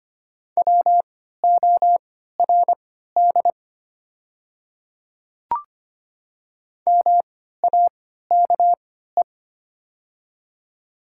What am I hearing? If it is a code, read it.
Morse code, heard as a keyed tone, about 25 wpm, spaced as for 14 wpm, 700 Hz. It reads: WORD MAKE
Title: 3x Character Spacing - Top 200 Words - 25wpm